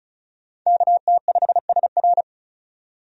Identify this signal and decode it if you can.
Morse code, heard as a keyed tone, about 36 wpm, 700 Hz. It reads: KT5SR
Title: USA Call Signs - 5 Characters - 35wpm